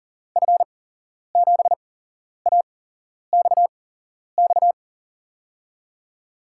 Morse code, heard as a keyed tone, about 43 wpm, spaced as for 11 wpm, 700 Hz. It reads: F7AXX
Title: USA Call Signs - 6 Characters - 8x Character Spacing - 40wpm